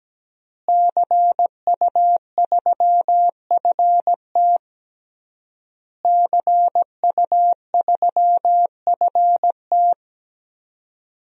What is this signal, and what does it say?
Morse code, heard as a keyed tone, about 17 wpm, 700 Hz. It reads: CU3FT CU3FT
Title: DXCC-100 Call Signs - 5 Characters - Repeated 3x - 17wpm